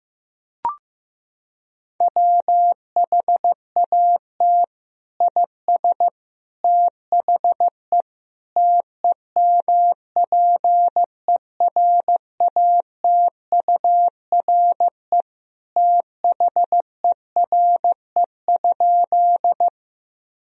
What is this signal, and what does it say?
Morse code, heard as a keyed tone, about 15 wpm, 700 Hz. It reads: WHAT IS THE TEMPERATURE THERE?
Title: Sentences from Top 1000 Words - Review Part 4 - 15wpm